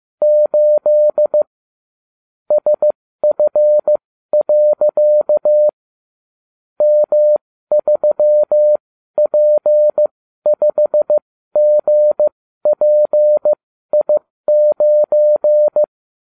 Morse code, heard as a keyed tone, about 15 wpm, spaced as for 12 wpm, 600 Hz. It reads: 8 SF. M3P5GPI9